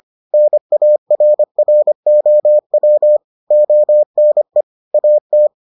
Morse code, heard as a keyed tone, about 25 wpm, 600 Hz. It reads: NARROW ONE AT